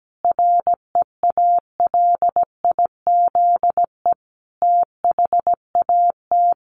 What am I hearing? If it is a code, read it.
Morse code, heard as a keyed tone, about 17 wpm, 700 Hz. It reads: REALIZE THAT
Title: Sentences from Top 1000 Words 17wpm